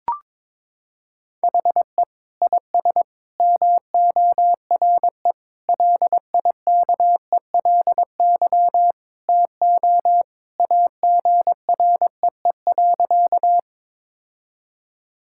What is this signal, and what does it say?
Morse code, heard as a keyed tone, about 22 wpm, 700 Hz. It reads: HE IS MORE LIKELY TO AGREE.
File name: practice / 22wpm / Sentences with 3-Word Phrases - 22wpm